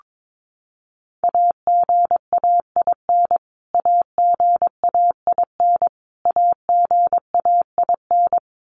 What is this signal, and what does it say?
Morse code, heard as a keyed tone, about 22 wpm, 700 Hz. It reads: AGAIN AGAIN AGAIN
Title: Top 200 Words Repeated 22wpm